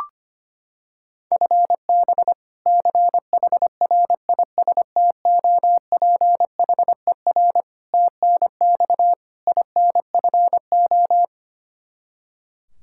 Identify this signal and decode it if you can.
Morse code, heard as a keyed tone, about 25 wpm, 700 Hz. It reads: FB CHRISTOPHER TNX INFO
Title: K1USN SST QSO Practice 25wpm